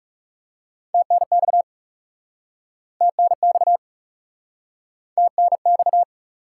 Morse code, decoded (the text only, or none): TNX TNX TNX